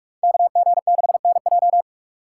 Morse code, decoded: KG6NJ

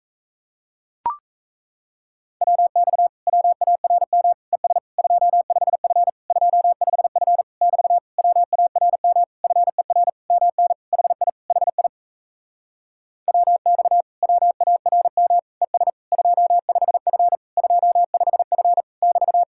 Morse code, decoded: WX WARM ES 25F 25F <BT> WARM FER MN HI HI WX WARM ES 25F 25F <BT>

